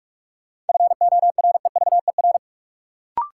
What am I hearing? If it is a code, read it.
Morse code, heard as a keyed tone, about 46 wpm, 700 Hz. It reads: FOREVER
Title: Sentences from Top 1600 Words - New Word Drill Alt - 45wpm